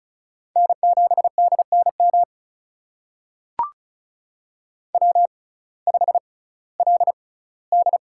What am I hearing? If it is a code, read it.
Morse code, heard as a keyed tone, about 36 wpm, spaced as for 19 wpm, 700 Hz. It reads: N7DNM W5LD
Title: USA Call Signs - 6x Character Spacing - 35wpm